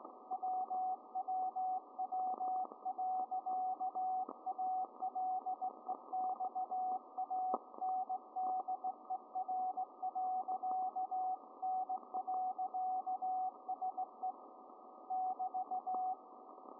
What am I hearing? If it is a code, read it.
Morse code, heard as a keyed tone, about 18 wpm, 750 Hz, with radio noise. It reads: WWW.ALEXANDER.N.SE <BT>